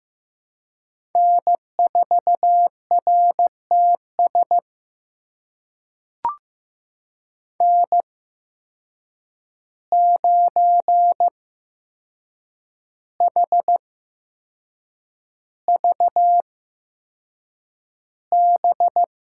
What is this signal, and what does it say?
Morse code, heard as a keyed tone, about 15 wpm, spaced as for 6 wpm, 700 Hz. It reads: N4RTS N9HVB
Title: USA Call Signs - 5 Characters - 8x Character Spacing - 15wpm